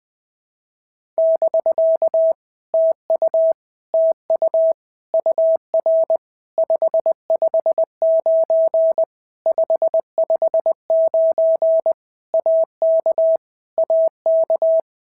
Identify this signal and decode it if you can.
Morse code, heard as a keyed tone, about 20 wpm, 650 Hz. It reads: <BK> TU TU UR 559 559 AK AK
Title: POTA QSO Practice - 20wpm